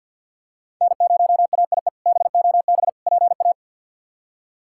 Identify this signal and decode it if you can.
Morse code, heard as a keyed tone, about 51 wpm, 700 Hz. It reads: N0AIE BOB PA
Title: K1USN SST QSO Practice - 50wpm